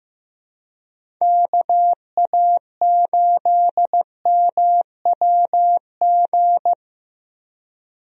Morse code, heard as a keyed tone, about 15 wpm, 700 Hz. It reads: KA8MWG
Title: USA Call Signs - 6 Characters - Repeated 2x - 15wpm